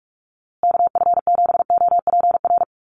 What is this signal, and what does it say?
Morse code, heard as a keyed tone, about 44 wpm, 700 Hz. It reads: KF7OPR
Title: USA Call Signs Repeated 2x 45wpm